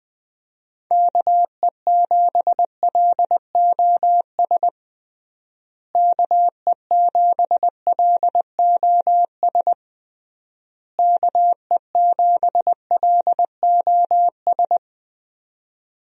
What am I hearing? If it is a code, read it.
Morse code, heard as a keyed tone, about 20 wpm, 700 Hz. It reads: KE7LOS KE7LOS KE7LOS
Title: USA Call Signs - 6 Characters - Repeated 3x - 20wpm